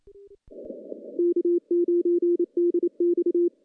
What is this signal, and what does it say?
Morse code, heard as a keyed tone, about 28 wpm, 350 Hz. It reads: K9DX
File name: contests / CW-SS-k5tr- -0100